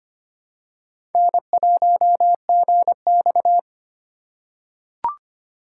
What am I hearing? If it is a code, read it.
Morse code, heard as a keyed tone, about 25 wpm, 700 Hz. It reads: N1GX
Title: USA Call Signs - 4 Characters - 4x Character Spacing - 25wpm